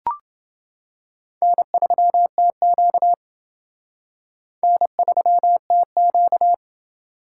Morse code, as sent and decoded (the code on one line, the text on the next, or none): -. ...-- - --.- / -. ...-- - --.-
N3TQ N3TQ